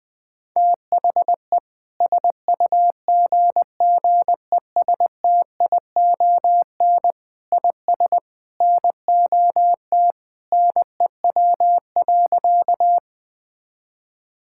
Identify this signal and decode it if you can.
Morse code, heard as a keyed tone, about 20 wpm, 700 Hz. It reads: THE SUGGESTION IS NOT NEW.
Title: Sentences from Top 1600 Words 20wpm